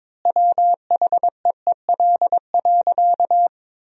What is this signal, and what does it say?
Morse code, heard as a keyed tone, about 22 wpm, 700 Hz. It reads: WHEEL.
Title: 3x Word-Spacing - Sentences from Top 500 Words - 22wpm